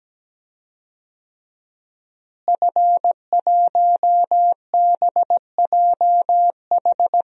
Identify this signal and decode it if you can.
Morse code, heard as a keyed tone, about 17 wpm, 700 Hz. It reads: F1BJH
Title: DXCC-100 Call Signs - 5 Characters - Repeated 2x - 17wpm